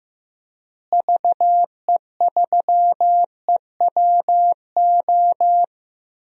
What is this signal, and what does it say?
Morse code, heard as a keyed tone, about 15 wpm, 700 Hz. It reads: VE3EWO
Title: DXCC-100 Call Signs - 6 Characters - Repeated 3x - 15wpm